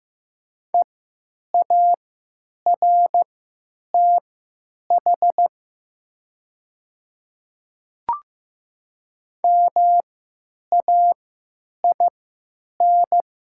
Morse code, heard as a keyed tone, about 15 wpm, spaced as for 9 wpm, 700 Hz. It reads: EARTH MAIN
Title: 3x Character Spacing - Top 300 Words - 15wpm